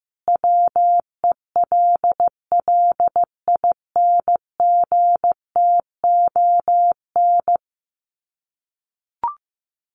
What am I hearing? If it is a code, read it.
Morse code, heard as a keyed tone, about 15 wpm, 700 Hz. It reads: WELLINGTON
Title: Common QSO Phrases 15wpm